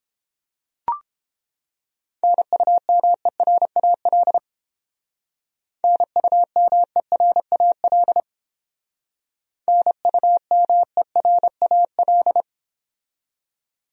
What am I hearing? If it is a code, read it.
Morse code, heard as a keyed tone, about 29 wpm, 700 Hz. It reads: NUMERAL NUMERAL NUMERAL